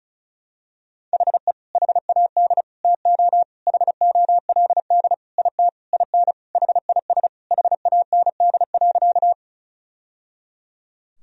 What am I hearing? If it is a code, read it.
Morse code, heard as a keyed tone, about 36 wpm, 700 Hz. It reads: HE HAD TO HOLD IT IN HIS HAND.
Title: Sentences from Top 1000 Words - Review Part 2 - 35wpm